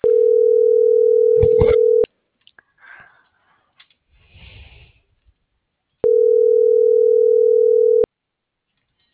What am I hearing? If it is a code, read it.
Ringback tone (North America), 440 + 480 Hz.